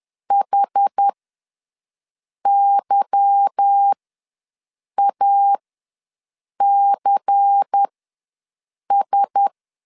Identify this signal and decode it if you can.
Morse code, heard as a keyed tone, about 10 wpm, spaced as for 6 wpm, 800 Hz. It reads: HYACS
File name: LEZIONI / 8 YJ45 / 8 YJ45 ws 10wpm 3843